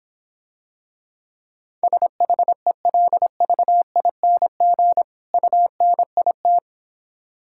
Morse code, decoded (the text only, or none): SHELVING UNIT